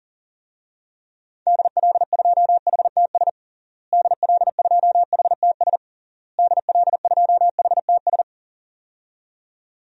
Morse code, heard as a keyed tone, about 40 wpm, 700 Hz. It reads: DL2HTS DL2HTS DL2HTS
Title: DXCC-100 Call Signs - 6 Characters - Repeated 3x - 40wpm